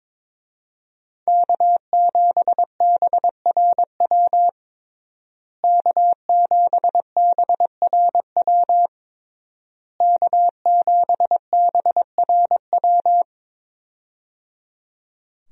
Morse code, heard as a keyed tone, about 21 wpm, 700 Hz. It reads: K7BRW K7BRW K7BRW
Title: USA Call Signs - 5 Characters - Repeated 3x - 22wpm